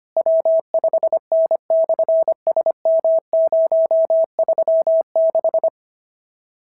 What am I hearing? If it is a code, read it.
Morse code, heard as a keyed tone, about 24 wpm, 650 Hz. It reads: W5N/SM036
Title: SOTA QSO Practice 25wpm